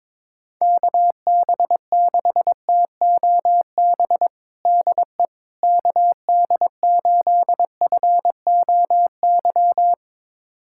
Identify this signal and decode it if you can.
Morse code, heard as a keyed tone, about 22 wpm, 700 Hz. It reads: KB6TOB DE KD8FOY
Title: USA Callsign DE Callsign 22wpm